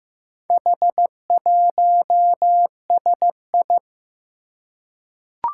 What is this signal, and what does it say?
Morse code, heard as a keyed tone, about 15 wpm, 700 Hz. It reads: H1SI
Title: Four Challenging Letter-Number Combinations 15wpm